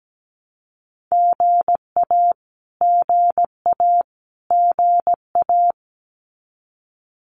Morse code, heard as a keyed tone, about 17 wpm, 700 Hz. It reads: GA GA GA
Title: Abbreviations Repeated 3x - 17wpm